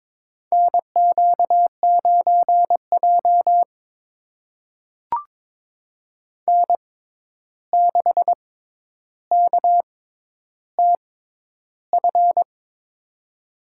Morse code, heard as a keyed tone, about 22 wpm, spaced as for 11 wpm, 700 Hz. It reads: NQ9J N6KTF
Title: USA Call Signs - 6x Character Spacing - 22wpm